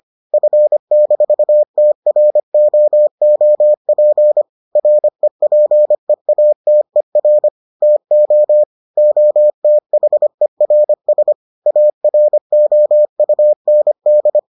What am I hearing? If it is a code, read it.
Morse code, heard as a keyed tone, about 25 wpm, 600 Hz. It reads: F-TROOP REPEATER TO OTHERS AROUND